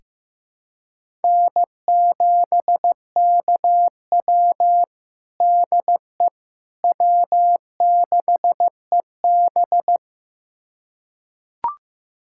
Morse code, heard as a keyed tone, about 15 wpm, 700 Hz. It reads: N7KW DE W6EB
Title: USA Callsign DE Callsign - 4 Characters - 15wpm